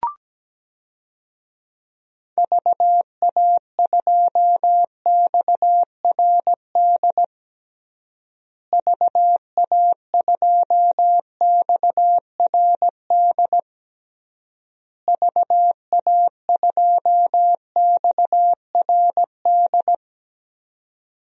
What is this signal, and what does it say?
Morse code, heard as a keyed tone, about 17 wpm, 700 Hz. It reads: VA2XRD VA2XRD VA2XRD
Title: DXCC-100 Call Signs - 6 Characters - Repeated 3x - 17wpm